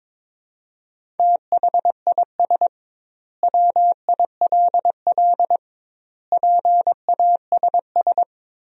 Morse code, heard as a keyed tone, about 22 wpm, 700 Hz. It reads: THIS WILL PASS
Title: 2x Word-Spacing - Sentences from Top 400 Words - 22wpm